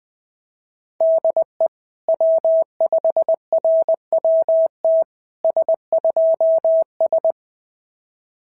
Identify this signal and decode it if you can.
Morse code, heard as a keyed tone, about 17 wpm, 650 Hz. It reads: DE W5RWT S2S
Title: SOTA QSO Practice 20wpm